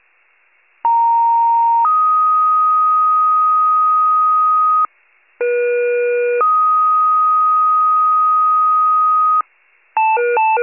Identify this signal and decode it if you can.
Two-tone page, heard as a one-second tone, then a three-second tone, about 930 then 1250 Hz, about 480 then 1250 Hz. It is heard from 0.8 to 9.4 s.
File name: wcalert